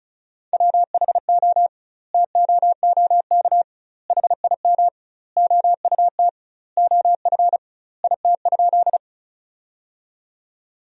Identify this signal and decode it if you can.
Morse code, heard as a keyed tone, about 36 wpm, 700 Hz. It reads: WHO TOOK HIM OUT OF IT?
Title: 2x Word-Spacing - Sentences from Top 300 Words - 35wpm